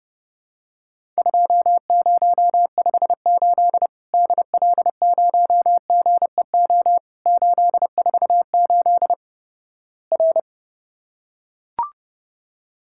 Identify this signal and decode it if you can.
Morse code, heard as a keyed tone, about 32 wpm, 700 Hz. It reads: 2058 DL0GEO 848 R
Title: Worked All Europe DX Contest - QTC Exchange - 30wpm